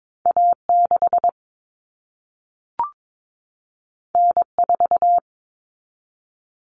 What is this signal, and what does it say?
Morse code, heard as a keyed tone, about 22 wpm, 700 Hz. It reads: A6 N4
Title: Common Cut Numbers - 2-Digit 22wpm